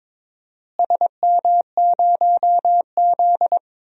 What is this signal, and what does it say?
Morse code, heard as a keyed tone, about 22 wpm, 700 Hz. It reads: SM0Z
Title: DXCC-100 Call Signs - 4 Characters - Repeated 3x - 22wpm